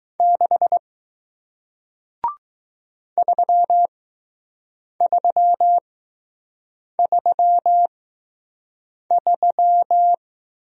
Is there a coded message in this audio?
Yes: Morse code from 0.2 to 10.1 s, heard as a keyed tone, about 20 wpm, spaced as for 6 wpm, 700 Hz. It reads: 6 3333